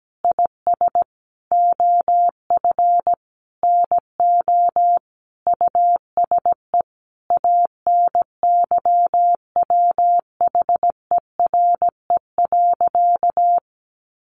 Morse code, heard as a keyed tone, about 17 wpm, 700 Hz. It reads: IS OF NO USE ANYWHERE.
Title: Sentences from Top 1400 Words 17wpm